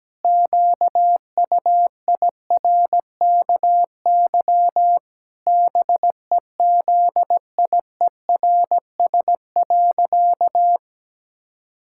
Morse code, heard as a keyed tone, about 17 wpm, 700 Hz. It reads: QUIRKY BEZIERS.
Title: Pangram Challenge v2 - 17wpm